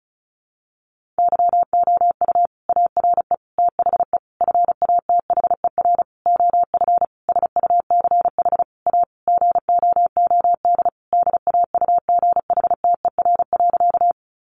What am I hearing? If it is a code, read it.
Morse code, heard as a keyed tone, about 34 wpm, 700 Hz. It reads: YOU ARE THE FATHER OF SUCH A GOOD DAUGHTER.
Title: Sentences from Top 1500 Words 35wpm